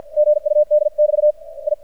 Morse code, heard as a keyed tone, about 53 wpm, 600 Hz. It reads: GUD <BT> I
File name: archived / CRSnet